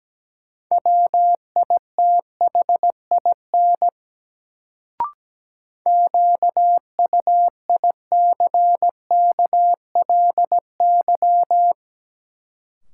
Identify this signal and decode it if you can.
Morse code, heard as a keyed tone, about 17 wpm, 700 Hz. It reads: WITHIN QUICKLY
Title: Sentences from Top 700 Words - New Word Drill - 17wpm